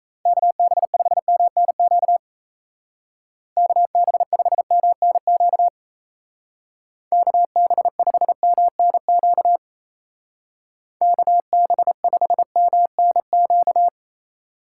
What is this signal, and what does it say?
Morse code, heard as a keyed tone, about 35 wpm, 700 Hz. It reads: KB5MNQ KB5MNQ KB5MNQ KB5MNQ
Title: Speed Racing - USA 2x3 Call Signs 28wpm